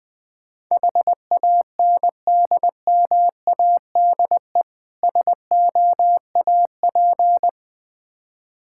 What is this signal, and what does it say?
Morse code, heard as a keyed tone, about 21 wpm, 700 Hz. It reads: HANDMADE SOAP
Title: Difficult Words with Hints - 20wpm